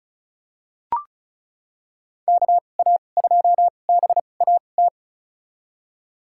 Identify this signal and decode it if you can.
Morse code, heard as a keyed tone, about 34 wpm, spaced as for 25 wpm, 700 Hz. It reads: KA2BAT
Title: USA Call Signs - 6 Characters - 2x Character Spacing - 35wpm